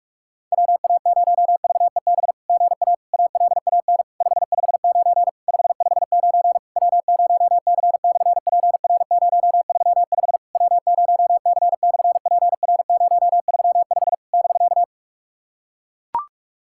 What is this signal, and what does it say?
Morse code, heard as a keyed tone, about 46 wpm, 690 Hz. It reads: WA0VEB GA ALAN 559 559 W0C/PR035 W0C/PR035 <BK>
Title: SOTA QSO Practice 45wpm